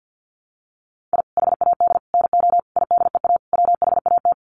Morse code, heard as a keyed tone, about 49 wpm, 700 Hz. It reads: I HAD NO IDEA WHAT
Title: Sets of 5 Words Encore 50wpm